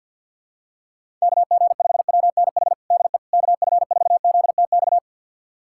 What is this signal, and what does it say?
Morse code, heard as a keyed tone, about 53 wpm, 700 Hz. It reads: KG5WNH DE KF4ZTX
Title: USA Callsign DE Callsign 50wpm